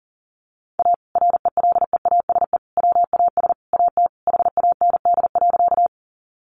Morse code, heard as a keyed tone, about 40 wpm, 700 Hz. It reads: A RELEASE WAS AT HAND.
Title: Sentences from Top 1100 Words 40wpm